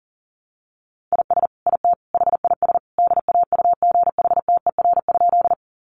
Morse code, heard as a keyed tone, about 37 wpm, 700 Hz. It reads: IS IT HIS DAUGHTER?
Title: Sentences from Top 1500 Words 40wpm